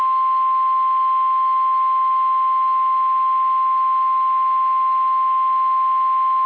Emergency Alert System (SAME) tones (the 1050 Hz attention tone). It is heard from 0.4 to 5.3 s.
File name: eas / KSDZKDJL / KSDZKDJL